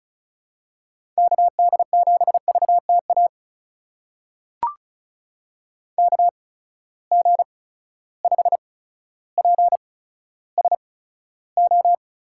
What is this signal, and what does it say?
Morse code, heard as a keyed tone, about 34 wpm, spaced as for 15 wpm, 700 Hz. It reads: KD7VTA KG5PSO
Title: USA Call Signs - 6 Characters - 8x Character Spacing - 35wpm